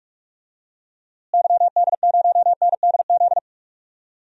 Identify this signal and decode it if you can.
Morse code, heard as a keyed tone, about 42 wpm, 700 Hz. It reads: YD0NDZ
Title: DXCC-100 Call Signs - 6 Characters - Repeated 3x - 45wpm